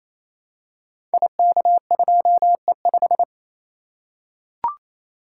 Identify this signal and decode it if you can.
Morse code, heard as a keyed tone, about 27 wpm, 700 Hz. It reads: IK2E5
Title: Code Groups - 5 Letter:Numbers - 28wpm